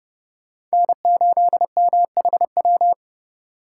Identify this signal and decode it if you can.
Morse code, heard as a keyed tone, about 37 wpm, 700 Hz. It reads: N8MHW
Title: Speed Racing - USA 1x3 Call Signs 30wpm